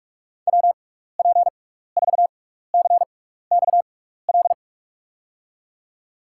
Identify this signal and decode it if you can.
Morse code, heard as a keyed tone, about 47 wpm, spaced as for 16 wpm, 700 Hz. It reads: WP4CXL
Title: USA Call Signs - 6 Characters - 6x Character Spacing - 45wpm